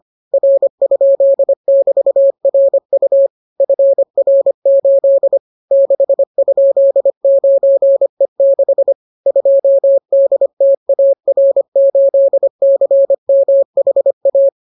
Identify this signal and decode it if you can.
Morse code, heard as a keyed tone, about 25 wpm, 550 Hz. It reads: R?<BT>RU FR8 6?9E6 2DTAR8CMHA